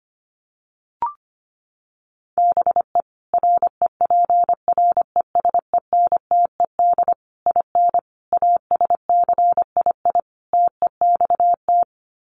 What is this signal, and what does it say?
Morse code, heard as a keyed tone, about 25 wpm, 700 Hz. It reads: BE REPRESENTED IN ASCII TEXT
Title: Sets of 5 Words 25wpm